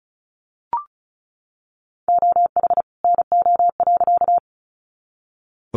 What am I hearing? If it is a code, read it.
Morse code, heard as a keyed tone, about 34 wpm, 700 Hz. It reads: OH NO.
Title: Additional Easy Phrases 35wpm